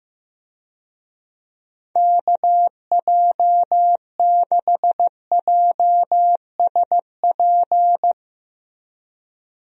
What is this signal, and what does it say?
Morse code, heard as a keyed tone, about 15 wpm, 700 Hz. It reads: KJ6JSP
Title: USA Call Signs - 6 Characters - Repeated 2x - 15wpm